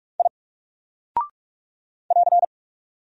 Morse code, heard as a keyed tone, about 46 wpm, 700 Hz. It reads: I <AR>
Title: Single Character (Letter:Number:Punctuation:Prosign) Alt 45wpm